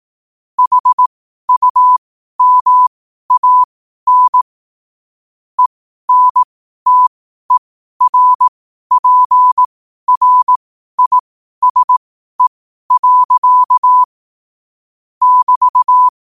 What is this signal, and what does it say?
Morse code, heard as a keyed tone, about 18 wpm, spaced as for 12 wpm, 1000 Hz. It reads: HUMAN ENTERPRISE. <BT>
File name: qotdfiles / 12wpm